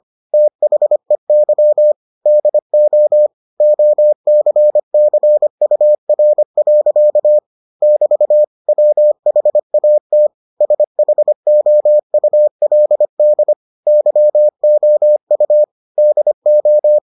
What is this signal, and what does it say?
Morse code, heard as a keyed tone, about 25 wpm, 600 Hz. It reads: THEY DO OCCUR. <BT> WHAT SHOULD YOU DO